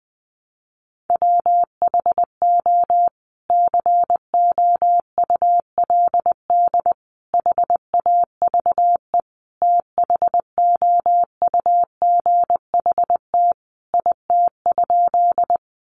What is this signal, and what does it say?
Morse code, heard as a keyed tone, about 20 wpm, 700 Hz. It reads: WHO COULD HAVE THOUGHT IT?
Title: Sentences from Top 300 Words 20wpm